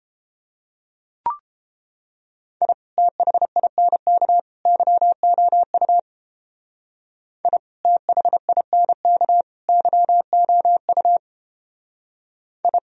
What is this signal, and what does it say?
Morse code, heard as a keyed tone, about 29 wpm, 700 Hz. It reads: I THINK YOU I THINK YOU I